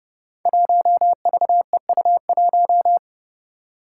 Morse code, heard as a keyed tone, about 30 wpm, 700 Hz. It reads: 1VEU1
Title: Five Challenging Letter-Number Combinations 30wpm